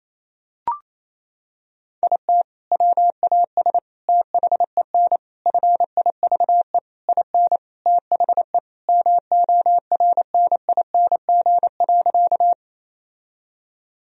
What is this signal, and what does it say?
Morse code, heard as a keyed tone, about 27 wpm, 700 Hz. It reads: IT WAS THEN FIVE IN THE MORNING.
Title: Sentences from Top 1000 Words - Review Part 2 - 28wpm